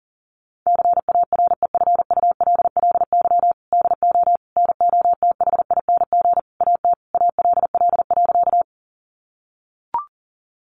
Morse code, heard as a keyed tone, about 40 wpm, 700 Hz. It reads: CAREFULLY DO NOTHING AT ALL.
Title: Sentences from Top 1300 Words 40wpm